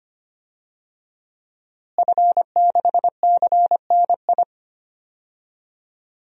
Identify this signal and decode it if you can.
Morse code, heard as a keyed tone, about 25 wpm, 700 Hz. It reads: F6CNI
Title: DXCC-100 Call Signs - 5 Characters - Repeated 2x - 25wpm